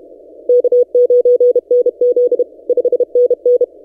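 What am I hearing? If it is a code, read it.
Morse code, heard as a keyed tone, about 32 wpm, 490 Hz. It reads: K9NZ 5NN
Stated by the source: Morse from DXpedition station A35X on 12 m, posted by N6ML.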